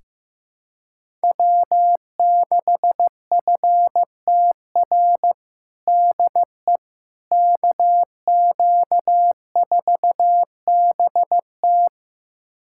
Morse code, heard as a keyed tone, about 15 wpm, 700 Hz. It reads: W6FTR DE KQ4BT
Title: USA Callsign DE Callsign - 5 Characters - 15wpm